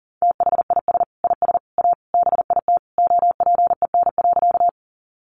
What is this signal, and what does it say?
Morse code, heard as a keyed tone, about 40 wpm, 700 Hz. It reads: THIS IS A BIT OPEN.
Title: Sentences from Top 900 Words 40wpm